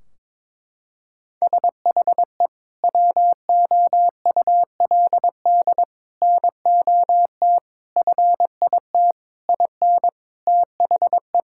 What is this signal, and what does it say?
Morse code, heard as a keyed tone, about 22 wpm, 700 Hz. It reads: SHE WOULD NOT FIT IN THE
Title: Sentences from Top 1600 Words 22wpm